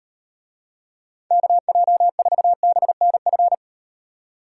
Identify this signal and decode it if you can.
Morse code, heard as a keyed tone, about 38 wpm, 700 Hz. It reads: KJ4BNF